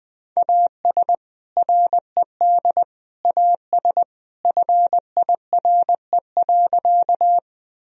Morse code, heard as a keyed tone, about 20 wpm, 700 Hz. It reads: AS RED AS FIRE.
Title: Sentences from Top 1000 Words - Review Part 2 - 20wpm